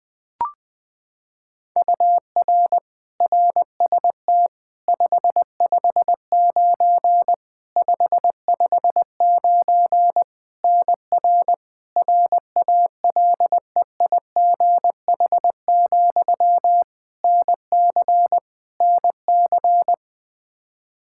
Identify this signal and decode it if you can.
Morse code, heard as a keyed tone, about 20 wpm, 700 Hz. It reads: UR RST 559 559 NR RALEIGH, NC NC
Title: K1USN SST QSO Practice 20wpm